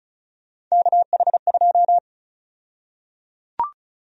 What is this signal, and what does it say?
Morse code, heard as a keyed tone, about 34 wpm, 700 Hz. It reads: KH2
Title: USA Call Sign Prefixes - 35wpm